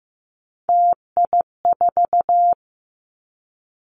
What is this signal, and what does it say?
Morse code, heard as a keyed tone, about 15 wpm, 700 Hz. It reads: TI4
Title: Callsign Prefixes for Top 100 DXCC - 15wpm